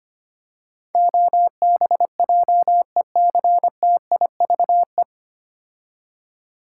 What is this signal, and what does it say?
Morse code, heard as a keyed tone, about 25 wpm, 700 Hz. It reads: OBJECTIVE
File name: practice / 25wpm / 9 Letter Words - First 3 Letters Given - 25wpm